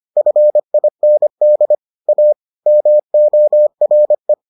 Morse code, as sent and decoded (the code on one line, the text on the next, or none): ..-. .. -. -.. / .- / -- --- .-. .
FIND A MORE